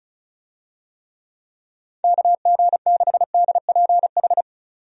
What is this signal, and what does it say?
Morse code, heard as a keyed tone, about 36 wpm, 700 Hz. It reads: KG6DPH
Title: USA Call Signs - 6 Characters - 35wpm